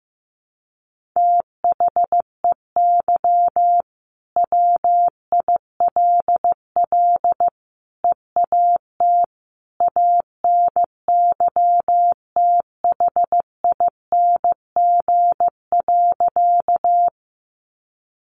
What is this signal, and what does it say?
Morse code, heard as a keyed tone, about 15 wpm, 700 Hz. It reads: THEY WILL EAT ANYTHING.
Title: Sentences from Top 1000 Words 15wpm